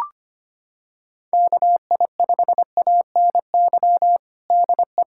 Morse code, heard as a keyed tone, about 25 wpm, 700 Hz. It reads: KI5ANY DE
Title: USA Callsign DE Callsign 25wpm